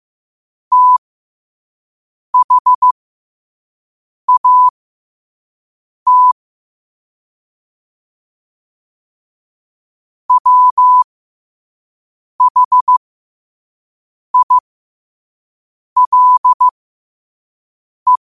Morse code, heard as a keyed tone, about 15 wpm, spaced as for 5 wpm, 1000 Hz. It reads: THAT WHILE
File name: qotdfiles / 5wpm